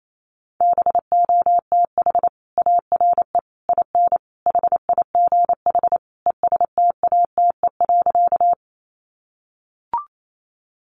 Morse code, heard as a keyed tone, about 28 wpm, 700 Hz. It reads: BOTH ARE IN HIGH ESTATE.
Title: Sentences from Top 1200 Words 28wpm